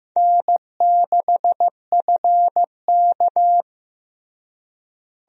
Morse code, heard as a keyed tone, about 15 wpm, 700 Hz. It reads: N6FK
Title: USA Call Signs - 4 Characters - Repeated 3x - 15wpm